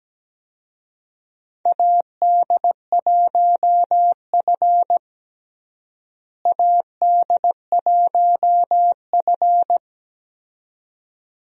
Morse code, heard as a keyed tone, about 17 wpm, 700 Hz. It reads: AD1F AD1F
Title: USA Call Signs - 4 Characters - Repeated 2x - 17wpm